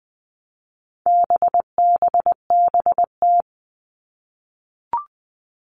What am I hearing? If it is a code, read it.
Morse code, heard as a keyed tone, about 20 wpm, 700 Hz. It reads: BBBT
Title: All Cut Numbers - 4-Digit 20wpm